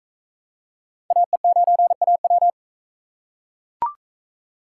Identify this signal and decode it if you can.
Morse code, heard as a keyed tone, about 41 wpm, 700 Hz. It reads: AE9AW